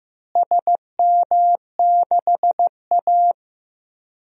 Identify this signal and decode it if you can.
Morse code, heard as a keyed tone, about 16 wpm, 700 Hz. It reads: SM6A
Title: DXCC-100 Call Signs - 4 Characters - Repeated 2x - 15wpm